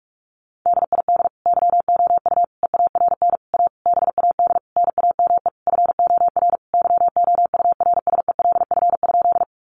Morse code, heard as a keyed tone, about 44 wpm, 700 Hz. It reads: DID YOU EARN A BAD NAME FOR YOURSELF?
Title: Sentences from Top 1500 Words 45wpm